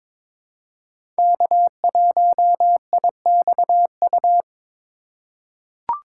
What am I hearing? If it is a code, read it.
Morse code, heard as a keyed tone, about 21 wpm, 700 Hz. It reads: K1IXU
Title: USA Call Signs - 5 Characters - 8x Character Spacing - 22wpm